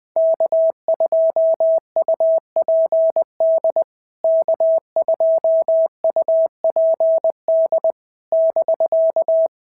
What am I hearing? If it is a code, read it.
Morse code, heard as a keyed tone, about 21 wpm, 650 Hz. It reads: K2UPD K2UPD <BK>
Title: POTA QSO Practice - 20wpm